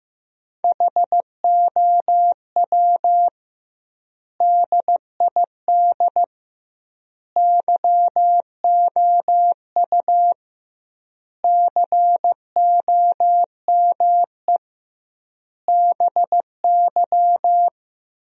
Morse code, heard as a keyed tone, about 16 wpm, 700 Hz. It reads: HOW DID YOU COME BY
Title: 2x Word-Spacing - Sentences from Top 100 Words - 15wpm